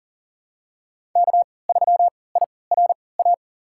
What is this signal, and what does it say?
Morse code, heard as a keyed tone, about 40 wpm, spaced as for 23 wpm, 700 Hz. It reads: K3IRA
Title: USA Call Signs - 5 Characters - 3x Character Spacing - 40wpm